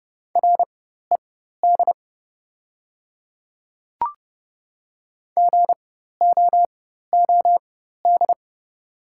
Morse code, heard as a keyed tone, about 30 wpm, spaced as for 14 wpm, 700 Hz. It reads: RED GOOD
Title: 4x Character Spacing - Top 500 Words - 30wpm